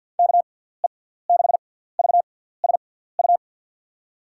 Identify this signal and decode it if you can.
Morse code, heard as a keyed tone, about 47 wpm, spaced as for 17 wpm, 700 Hz. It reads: KE6VSU
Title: USA Call Signs - 6x Character Spacing - 50wpm